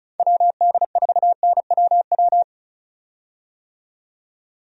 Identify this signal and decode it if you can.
Morse code, heard as a keyed tone, about 34 wpm, 700 Hz. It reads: WD4NWW
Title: USA Call Signs - 6 Characters - 35wpm